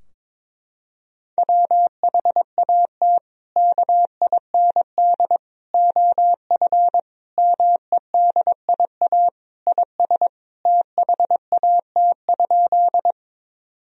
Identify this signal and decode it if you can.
Morse code, heard as a keyed tone, about 22 wpm, 700 Hz. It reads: WHAT KIND OF MEDIA IS THAT?
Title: Sentences from Top 1000 Words - Review Part 4 - 22wpm